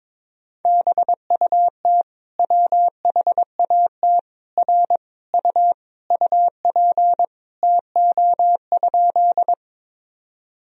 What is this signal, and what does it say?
Morse code, heard as a keyed tone, about 22 wpm, 700 Hz. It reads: BUT WHAT R U UP TO?
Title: Sentences from Top 100 Words and Abbreviations 22wpm